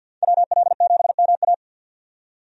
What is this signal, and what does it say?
Morse code, heard as a keyed tone, about 52 wpm, 700 Hz. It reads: WL7MA
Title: USA Call Signs - 5 Characters - 50wpm